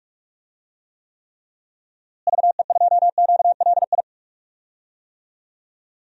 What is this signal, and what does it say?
Morse code, heard as a keyed tone, about 46 wpm, 700 Hz. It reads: VE2QLI
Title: DXCC-100 Call Signs - 6 Characters - Repeated 3x - 45wpm